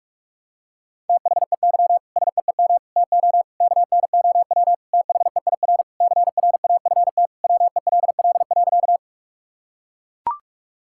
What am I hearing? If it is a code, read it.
Morse code, heard as a keyed tone, about 41 wpm, 700 Hz. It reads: THEY SEEM TO KNOW THEIR CRAFT WELL.
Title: Sentences from Top 1000 Words - Review Part 6 - 45wpm